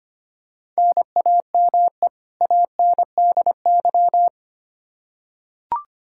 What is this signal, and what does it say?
Morse code, heard as a keyed tone, about 25 wpm, 700 Hz. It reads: NAME ANDY
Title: Practice QSO - Exchange 1 - Beginning - 25wpm